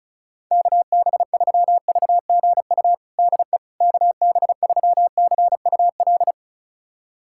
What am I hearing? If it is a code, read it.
Morse code, heard as a keyed tone, about 36 wpm, 700 Hz. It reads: KB3VGU DE KB3CUL
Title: USA Callsign DE Callsign 35wpm